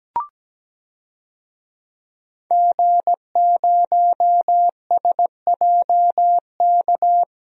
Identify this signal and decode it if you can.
Morse code, heard as a keyed tone, about 17 wpm, 700 Hz. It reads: G0SJK
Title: DXCC-100 Call Signs - 5 Characters - 17wpm